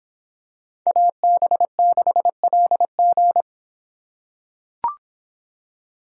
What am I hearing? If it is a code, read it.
Morse code, heard as a keyed tone, about 24 wpm, 700 Hz. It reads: AB6LG